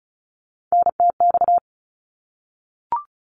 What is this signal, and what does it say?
Morse code, heard as a keyed tone, about 36 wpm, 700 Hz. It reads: NTX
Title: ARRL Sections 35wpm